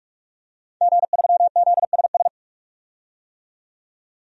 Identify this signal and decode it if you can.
Morse code, heard as a keyed tone, about 48 wpm, 700 Hz. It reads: G3ZSS
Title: DXCC-100 Call Signs - 5 Characters - 45wpm